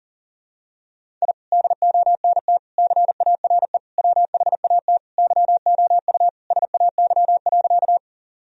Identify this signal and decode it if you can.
Morse code, heard as a keyed tone, about 40 wpm, 700 Hz. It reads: I DONT CARE WHAT YOU SAY.
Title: Sentences from Top 1000 Words - Review Part 1 - 40wpm